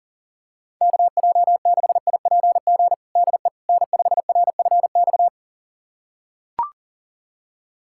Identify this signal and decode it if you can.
Morse code, heard as a keyed tone, about 37 wpm, 700 Hz. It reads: KJ6IPG DE N5RFX
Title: USA Callsign DE Callsign 40wpm